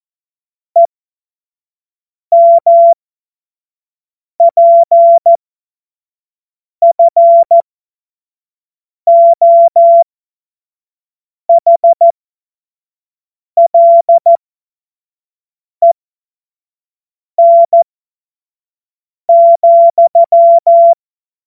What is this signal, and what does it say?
Morse code, heard as a keyed tone, about 14 wpm, spaced as for 5 wpm, 680 Hz. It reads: EMPFOHLEN,
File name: Morsekurs / Witze / Witz 22. August 2023 25 F70